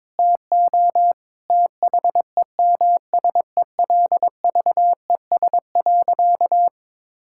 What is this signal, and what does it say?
Morse code, heard as a keyed tone, about 22 wpm, 700 Hz. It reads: TO THEMSELVES.
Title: Sentences from Top 1000 Words - Review Part 5 - 22wpm